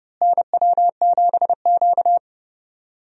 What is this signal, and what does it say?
Morse code, heard as a keyed tone, about 30 wpm, 700 Hz. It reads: NW7Q
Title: USA Call Signs - 4 Characters - Repeated 3x - 30wpm